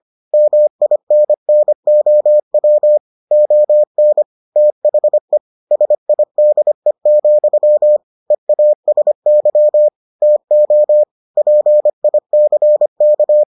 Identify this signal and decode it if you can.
Morse code, heard as a keyed tone, about 25 wpm, 600 Hz. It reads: MINNOW ON THE SIDE, EASY TO PICK